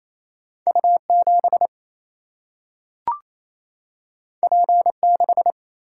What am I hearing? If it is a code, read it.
Morse code, heard as a keyed tone, about 28 wpm, 700 Hz. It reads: U7 P6
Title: Single-Letter Single-Number 28wpm